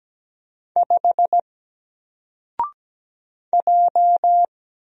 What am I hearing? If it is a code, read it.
Morse code, heard as a keyed tone, about 17 wpm, 700 Hz. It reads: 5 J
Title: Single Letter-Number Alt 17wpm